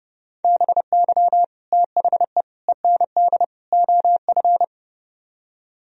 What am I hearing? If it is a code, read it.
Morse code, heard as a keyed tone, about 35 wpm, 700 Hz. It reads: BY THE END OF
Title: Speed-Racing - Sets of 4 Words 30wpm